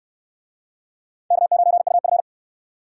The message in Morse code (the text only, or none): K9UF